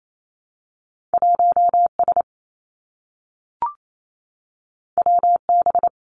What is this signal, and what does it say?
Morse code, heard as a keyed tone, about 28 wpm, 700 Hz. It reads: 1 S W B
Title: Two Challenging Letter-Number Combinations 28wpm